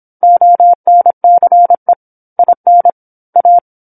Morse code, heard as a keyed tone, about 27 wpm, 700 Hz. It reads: ONCE IN A
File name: 08-Chapter-08 26 WPM